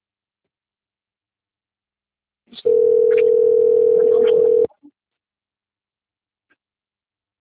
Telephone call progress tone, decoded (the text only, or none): ringback tone